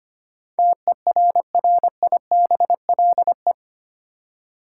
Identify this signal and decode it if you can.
Morse code, heard as a keyed tone, about 25 wpm, 700 Hz. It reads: TERRIBLE
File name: Eight Letter Words - 25wpm